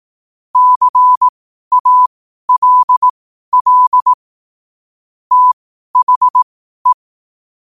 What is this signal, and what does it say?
Morse code, heard as a keyed tone, about 18 wpm, spaced as for 12 wpm, 1000 Hz. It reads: CALL THE
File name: qotdfiles / 12wpm